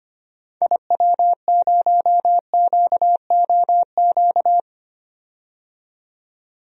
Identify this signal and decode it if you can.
Morse code, heard as a keyed tone, about 25 wpm, 700 Hz. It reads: IW0QOQ
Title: DXCC-100 Call Signs - 6 Characters - 25wpm